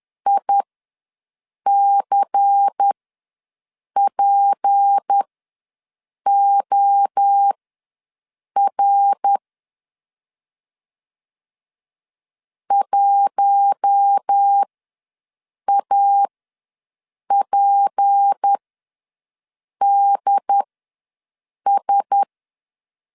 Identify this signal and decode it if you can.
Morse code, heard as a keyed tone, about 10 wpm, spaced as for 6 wpm, 800 Hz. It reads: ICPOR 1APDS
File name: LEZIONI / 6 PKX0 / 6 P ws 10wpm 3751